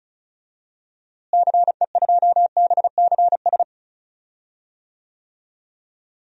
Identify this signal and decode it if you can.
Morse code, heard as a keyed tone, about 36 wpm, 700 Hz. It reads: CE2BCS
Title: DXCC-100 Call Signs - 6 Characters - Repeated 2x - 35wpm